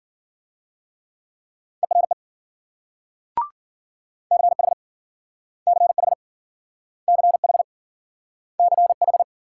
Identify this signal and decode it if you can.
Morse code, heard as a keyed tone, about 53 wpm, 700 Hz. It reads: ERE CH CH CH CH